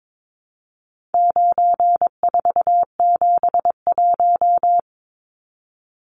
Morse code, heard as a keyed tone, about 22 wpm, 700 Hz. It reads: 9471
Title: Numbers - 4-Digit 22wpm